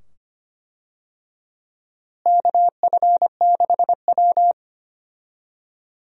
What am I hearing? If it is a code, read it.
Morse code, heard as a keyed tone, about 25 wpm, 700 Hz. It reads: KF6W
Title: USA Call Signs - 4 Characters - Repeated 2x - 25wpm